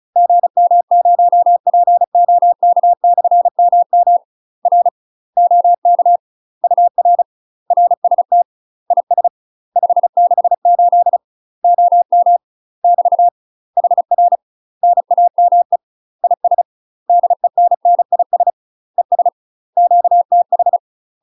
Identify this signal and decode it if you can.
Morse code, heard as a keyed tone, about 34 wpm, 700 Hz. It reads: GM0POK/MM R OK UR RST IS 568 OM <BT> HR NAME IS DENNIS ES QTH
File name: UK MASTER QSOs TRAINING/35WPM